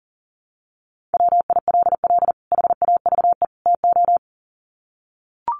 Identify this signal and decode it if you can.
Morse code, heard as a keyed tone, about 40 wpm, 700 Hz. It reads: WILL HAVE TO
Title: Sets of 3 Words 40wpm